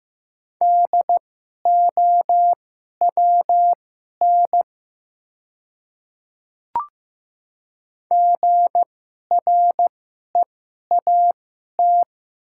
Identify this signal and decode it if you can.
Morse code, heard as a keyed tone, about 15 wpm, spaced as for 11 wpm, 700 Hz. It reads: DOWN GREAT
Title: 2x Character Spacing - Top 500 Words - 15wpm